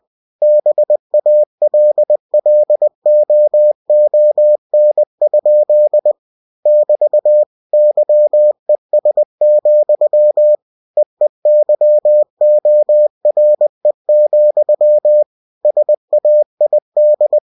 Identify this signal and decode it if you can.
Morse code, heard as a keyed tone, about 20 wpm, 600 Hz. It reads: BALLOON? <BT> YES, EEYORE, SAID
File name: cw-books / winnie-the-pooh / winnie-20wpm-page-0056